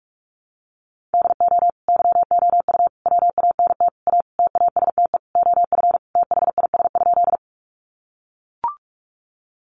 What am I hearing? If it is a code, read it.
Morse code, heard as a keyed tone, about 44 wpm, 700 Hz. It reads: DO YOU WANT A TASTE OF THIS?
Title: Sentences from Top 1100 Words 45wpm